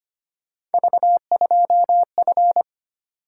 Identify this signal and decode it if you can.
Morse code, heard as a keyed tone, about 25 wpm, 700 Hz. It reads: V2F
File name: Three Letter-Number Combinations - 25wpm